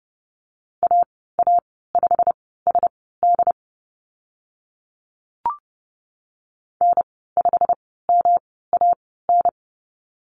Morse code, heard as a keyed tone, about 30 wpm, spaced as for 17 wpm, 700 Hz. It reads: AA5SD N5MAN
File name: Lesson 19g - Letter w - Call Signs 30wpm